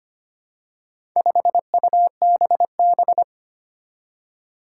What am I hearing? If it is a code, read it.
Morse code, heard as a keyed tone, about 25 wpm, 700 Hz. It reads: 5UBB
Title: Four Challenging Letter-Number Combinations 25wpm